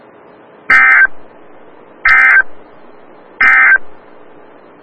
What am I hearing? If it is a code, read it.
Emergency Alert System (SAME) tones, heard as harsh buzzing bursts. It is heard from 0.6 to 3.9 s.
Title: Des Moines, IA NOAA Weather Radio - WXL57